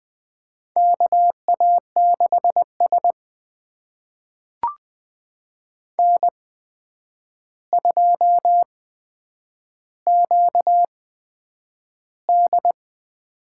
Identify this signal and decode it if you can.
Morse code, heard as a keyed tone, about 20 wpm, spaced as for 9 wpm, 700 Hz. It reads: KA6S N2QD
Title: USA Call Signs - 4 Characters - 8x Character Spacing - 20wpm